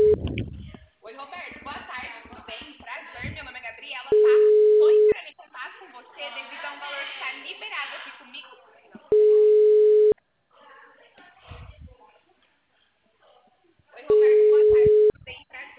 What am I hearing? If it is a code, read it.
Ringback tone (Europe, 425 Hz), with speech.